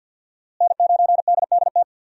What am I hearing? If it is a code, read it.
Morse code, heard as a keyed tone, about 51 wpm, 700 Hz. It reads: N9DDT
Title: USA Call Signs - 5 Characters - Repeated 3x - 50wpm